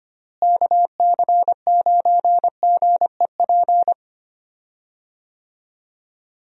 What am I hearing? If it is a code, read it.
Morse code, heard as a keyed tone, about 25 wpm, 700 Hz. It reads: KC9GEP
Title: USA Call Signs - 6 Characters - Repeated 2x - 25wpm